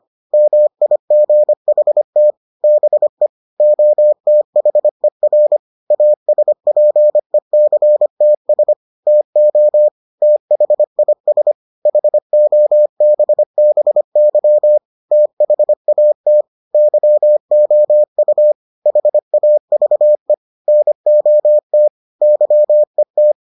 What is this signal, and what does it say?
Morse code, heard as a keyed tone, about 25 wpm, 600 Hz. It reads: MIGHT BE OTHER ASPECTS TO THIS HOBBY THAT YOU HAVE NOT YET